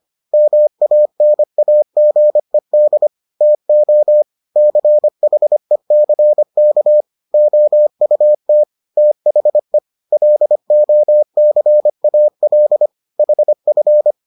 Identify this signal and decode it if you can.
Morse code, heard as a keyed tone, about 25 wpm, 600 Hz. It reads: MANAGED TO CHECK OUT THE LOCAL HF